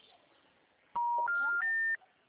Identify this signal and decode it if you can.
Special information tone.